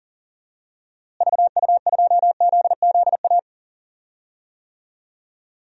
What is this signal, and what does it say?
Morse code, heard as a keyed tone, about 39 wpm, 700 Hz. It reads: VU2ZZA
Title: DXCC-100 Call Signs - 6 Characters - Repeated 3x - 40wpm